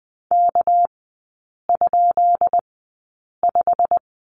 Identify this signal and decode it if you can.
Morse code, heard as a keyed tone, about 20 wpm, spaced as for 8 wpm, 700 Hz. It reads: K?5
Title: 198 Lesson 32e - Letter k - Three Characters 20wpm